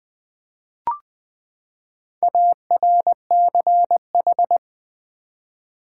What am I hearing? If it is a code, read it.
Morse code, heard as a keyed tone, about 20 wpm, 700 Hz. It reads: ARCH
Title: Build A Word - 4 Letters At A Time - 12-Letter Words - 20wpm